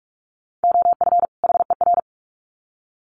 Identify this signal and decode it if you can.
Morse code, heard as a keyed tone, about 44 wpm, 700 Hz. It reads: OF HER
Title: Sets of 2 Words 45wpm